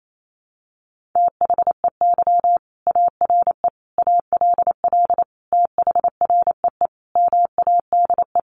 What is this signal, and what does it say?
Morse code, heard as a keyed tone, about 28 wpm, 700 Hz. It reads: THEY ARE ALL THREE MADE
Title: Sentences from Top 1500 Words 28wpm